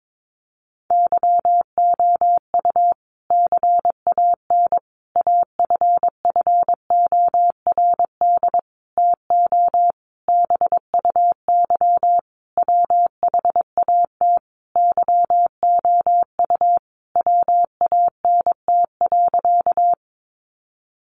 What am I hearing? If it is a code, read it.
Morse code, heard as a keyed tone, about 22 wpm, 700 Hz. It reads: YOU CAN AFFORD TO BUY WHAT YOU WANT.
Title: Sentences from Top 1100 Words 22wpm